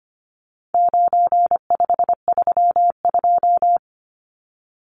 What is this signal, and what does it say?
Morse code, heard as a keyed tone, about 25 wpm, 700 Hz. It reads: 9532
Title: Numbers - 4-Digit 25wpm